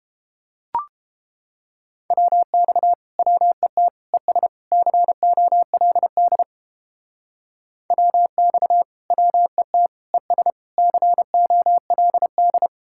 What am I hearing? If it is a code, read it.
Morse code, heard as a keyed tone, about 31 wpm, 700 Hz. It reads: WX WET ES COLD WX WET ES COLD